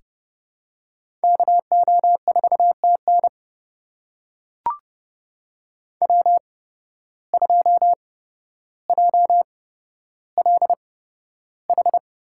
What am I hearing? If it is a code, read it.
Morse code, heard as a keyed tone, about 30 wpm, spaced as for 13 wpm, 700 Hz. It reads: KO4TN W2JLH
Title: USA Call Signs - 8x Character Spacing - 30wpm